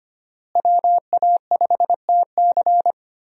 Morse code, heard as a keyed tone, about 25 wpm, 700 Hz. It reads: WA5TC
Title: USA Call Signs - 5 Characters - Repeated 3x - 25wpm